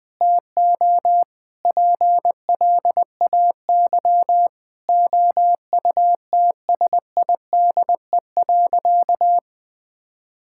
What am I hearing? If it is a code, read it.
Morse code, heard as a keyed tone, about 20 wpm, 700 Hz. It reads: TO PLAY OUTSIDE.
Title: Sentences with 5-Word Phrases - 20wpm